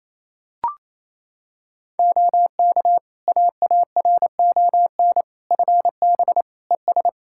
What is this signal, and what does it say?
Morse code, heard as a keyed tone, about 28 wpm, 700 Hz. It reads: OK AARON FB ES
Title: Practice QSO - Exchange 2 - Info - 28wpm